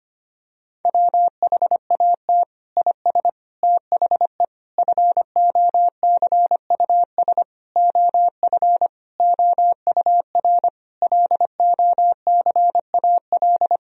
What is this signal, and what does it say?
Morse code, heard as a keyed tone, about 25 wpm, 700 Hz. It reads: WHAT IS THE FOCUS OF OUR LOCAL
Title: Sentences from Top 1000 Words - Review Part 4 - 25wpm